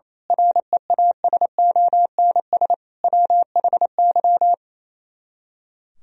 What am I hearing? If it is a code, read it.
Morse code, heard as a keyed tone, about 27 wpm, 700 Hz. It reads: REASONS WHY
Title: Speed-Racing - Sets of 5 Words 28wpm